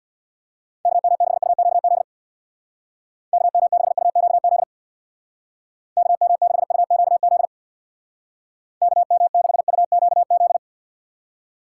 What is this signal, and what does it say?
Morse code, heard as a keyed tone, about 63 wpm, 700 Hz. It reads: KM6UQZ KM6UQZ KM6UQZ KM6UQZ